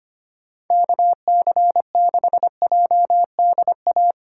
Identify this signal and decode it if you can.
Morse code, heard as a keyed tone, about 31 wpm, 700 Hz. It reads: KC6JDA
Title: Speed Racing - USA 2x3 Call Signs 25wpm